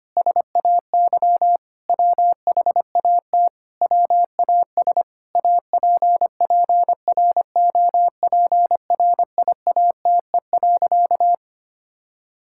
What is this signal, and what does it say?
Morse code, heard as a keyed tone, about 25 wpm, 700 Hz. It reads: SAY WHAT WAS APPROPRIATE.
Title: Sentences from Top 1000 Words - Review Part 5 - 25wpm